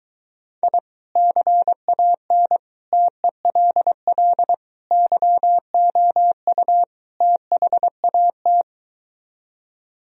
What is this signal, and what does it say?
Morse code, heard as a keyed tone, about 24 wpm, 700 Hz. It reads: I CAN TELL YOU THAT